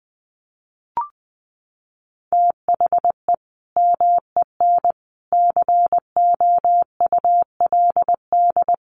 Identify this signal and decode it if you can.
Morse code, heard as a keyed tone, about 20 wpm, 700 Hz. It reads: THE MEN COULD
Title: Sentences from Top 1200 Words 20wpm